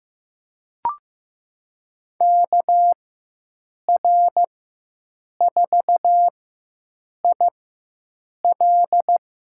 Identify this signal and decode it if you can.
Morse code, heard as a keyed tone, about 15 wpm, spaced as for 7 wpm, 700 Hz. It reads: KR4IL
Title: USA Call Signs - 5 Characters - 4x Character Spacing - 15wpm